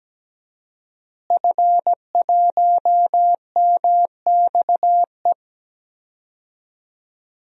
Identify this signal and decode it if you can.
Morse code, heard as a keyed tone, about 17 wpm, 700 Hz. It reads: F1MXE
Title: DXCC-100 Call Signs - 5 Characters - Repeated 3x - 17wpm